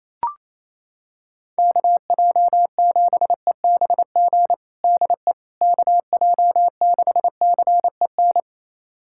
Morse code, heard as a keyed tone, about 28 wpm, 700 Hz. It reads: KJ7EBG DE KJ6CEN
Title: USA Callsign DE Callsign - 6 Characters - 28wpm